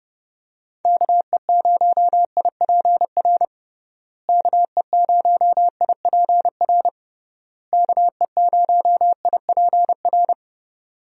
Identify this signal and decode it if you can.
Morse code, heard as a keyed tone, about 30 wpm, 700 Hz. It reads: KE0IPR KE0IPR KE0IPR
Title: USA Call Signs - 6 Characters - Repeated 3x - 30wpm